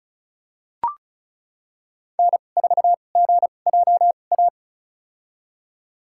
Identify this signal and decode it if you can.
Morse code, heard as a keyed tone, about 34 wpm, spaced as for 25 wpm, 700 Hz. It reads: N4GJA
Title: USA Call Signs - 5 Characters - 2x Character Spacing - 35wpm